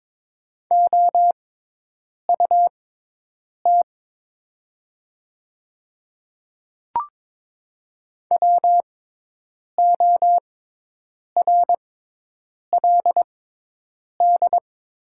Morse code, heard as a keyed tone, about 22 wpm, spaced as for 8 wpm, 700 Hz. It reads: OUT WORLD
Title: 6x Character Spacing - Top 200 Words - 22wpm